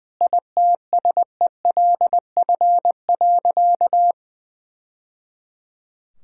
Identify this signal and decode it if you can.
Morse code, heard as a keyed tone, about 20 wpm, 700 Hz. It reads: ITSELF.
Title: Sentences from Top 1000 Words - Review Part 5 - 20wpm